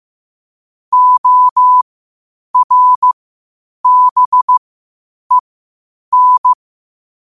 Morse code, heard as a keyed tone, about 15 wpm, spaced as for 8 wpm, 1000 Hz. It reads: ORBEN
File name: qotdfiles / 8wpm